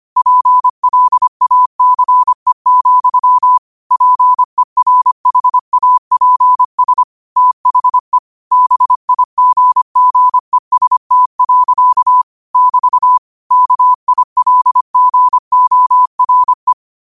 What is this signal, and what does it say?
Morse code, heard as a keyed tone, about 25 wpm, 1000 Hz. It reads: PLACE, PERHAPS THE BIGGEST. <BT> KILGORE